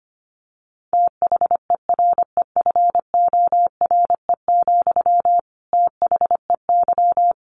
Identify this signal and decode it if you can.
Morse code, heard as a keyed tone, about 25 wpm, 700 Hz. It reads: THEREFORE, THEY
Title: Sentences from Top 800 Words 25wpm